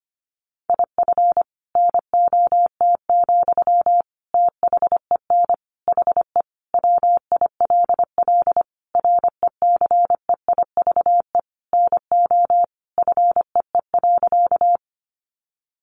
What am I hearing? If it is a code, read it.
Morse code, heard as a keyed tone, about 25 wpm, 700 Hz. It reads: IF NOT, THEN HE WILL RECEIVE NO FEE.
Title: Sentences from Top 1500 Words 25wpm